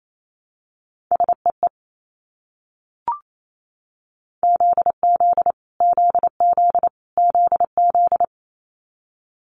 Morse code, heard as a keyed tone, about 28 wpm, 700 Hz. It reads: SEE ZZ ZZ ZZ
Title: Multi-letter Phonemes Repeated 3x 28wpm